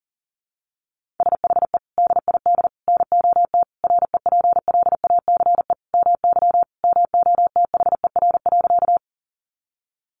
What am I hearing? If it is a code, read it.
Morse code, heard as a keyed tone, about 40 wpm, 700 Hz. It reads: SHE DID NOT REPLACE MY MOTHER.
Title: Sentences from Top 1400 Words 40wpm